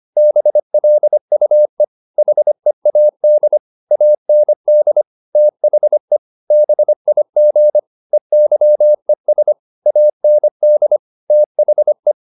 Morse code, heard as a keyed tone, about 25 wpm, 600 Hz. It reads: BLUE HEAD AND THE BIG EYES AND THE